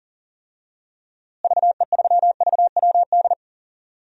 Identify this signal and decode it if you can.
Morse code, heard as a keyed tone, about 39 wpm, 700 Hz. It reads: VE3VWD
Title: DXCC-100 Call Signs - 6 Characters - Repeated 3x - 40wpm